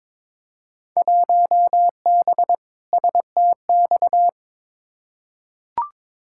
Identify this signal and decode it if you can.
Morse code, heard as a keyed tone, about 22 wpm, 700 Hz. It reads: 1B STX
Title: Field Day Exchanges - 22wpm